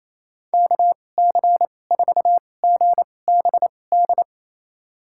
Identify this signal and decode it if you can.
Morse code, heard as a keyed tone, about 27 wpm, spaced as for 20 wpm, 700 Hz. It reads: KC4GBD
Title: USA Call Signs - 2x Character Spacing - 28wpm